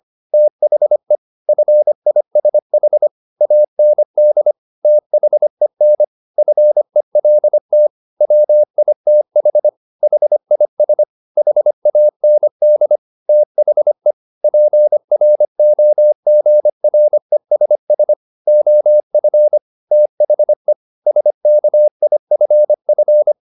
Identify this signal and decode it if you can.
Morse code, heard as a keyed tone, about 25 wpm, 600 Hz. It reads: THE FISH AND THEN FELT WITH HIS HAND THE PROGRESS OF THE SKIFF
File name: the old man and the sea part 1 25wpm